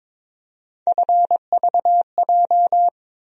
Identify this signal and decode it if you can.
Morse code, heard as a keyed tone, about 22 wpm, 700 Hz. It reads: FVJ
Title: USA Call Sign Postfixes - 3 Letters 22wpm